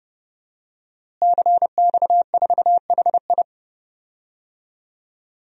Morse code, heard as a keyed tone, about 31 wpm, 700 Hz. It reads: CX4HI
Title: DXCC-100 Call Signs - 5 Characters - Repeated 2x - 30wpm